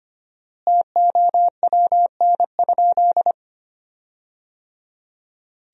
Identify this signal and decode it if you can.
Morse code, heard as a keyed tone, about 25 wpm, 700 Hz. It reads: TOWN?
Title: 4x Word-Spacing - Sentences from Top 500 Words - 25wpm